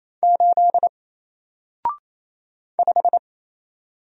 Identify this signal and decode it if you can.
Morse code, heard as a keyed tone, about 28 wpm, 700 Hz. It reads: 8 5
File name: Single-Letter-Number-28wpm